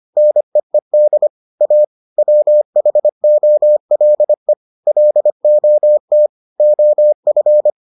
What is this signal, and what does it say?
Morse code, heard as a keyed tone, about 25 wpm, 600 Hz. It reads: NEED A WHOLE LOT OF